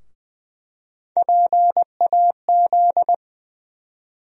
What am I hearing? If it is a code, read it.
Morse code, heard as a keyed tone, about 20 wpm, 700 Hz. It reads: PAZ
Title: USA Call Sign Postfixes - 3 Letters 20wpm